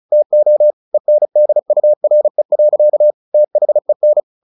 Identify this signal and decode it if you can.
Morse code, heard as a keyed tone, about 34 wpm, 600 Hz. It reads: TO ENDURE. THEN
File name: the old man and the sea / the old man and the sea 1 35wpm